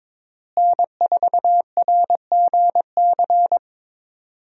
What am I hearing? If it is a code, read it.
Morse code, heard as a keyed tone, about 22 wpm, 700 Hz. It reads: N4RGC
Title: USA Call Signs - 5 Characters - Repeated 3x - 22wpm